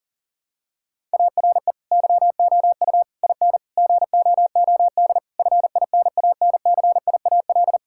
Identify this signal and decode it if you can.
Morse code, heard as a keyed tone, about 37 wpm, 700 Hz. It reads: ARE YOU IN GOOD FINANCIAL
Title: Sentences from Top 1000 Words - Review Part 5 - 40wpm